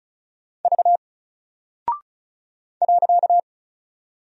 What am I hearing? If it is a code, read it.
Morse code, heard as a keyed tone, about 36 wpm, 700 Hz. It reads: V .
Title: Single Character (Letter:Number:Punctuation:Prosign) 35wpm